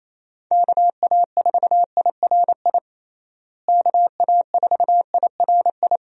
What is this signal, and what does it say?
Morse code, heard as a keyed tone, about 27 wpm, 700 Hz. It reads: KA4IRI KA4IRI
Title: USA Call Signs - 6 Characters - Repeated 2x - 28wpm